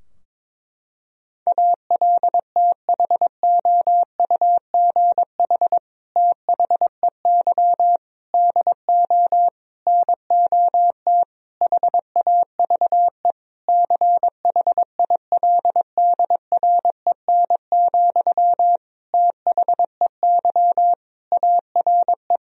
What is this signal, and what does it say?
Morse code, heard as a keyed tone, about 22 wpm, 700 Hz. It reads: ALTHOUGH THEY DO NOT HAVE CHILDREN, THEY ARE
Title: Sentences from Top 1000 Words - Review Part 3 - 22wpm